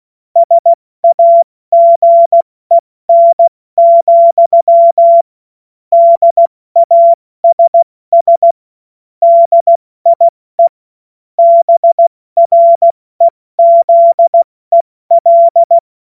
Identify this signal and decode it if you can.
Morse code, heard as a keyed tone, about 16 wpm, spaced as for 14 wpm, 680 Hz. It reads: SAGEN, DASS DIE BREZEL
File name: Morsekurs / Witze / Witz 15. November 2022 70 F80